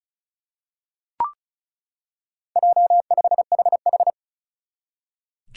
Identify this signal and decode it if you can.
Morse code, heard as a keyed tone, about 36 wpm, 700 Hz. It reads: J5HH
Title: Four Challenging Letter-Number Combinations 35wpm